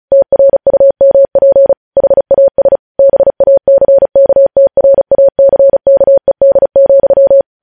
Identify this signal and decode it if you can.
Morse code, heard as a keyed tone, about 36 wpm, 550 Hz. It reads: TRUMP HAS BACKTRACKED,
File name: podcast / News headlines-1x-35x35